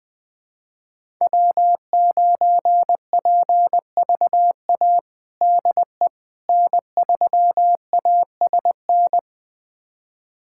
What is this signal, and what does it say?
Morse code, heard as a keyed tone, about 20 wpm, 700 Hz. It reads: W9PVA DE N3ASN
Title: USA Callsign DE Callsign - 5 Characters - 20wpm